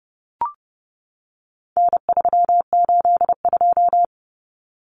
Morse code, heard as a keyed tone, about 30 wpm, 700 Hz. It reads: N382
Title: Common Cut Numbers - 4-Digit 30wpm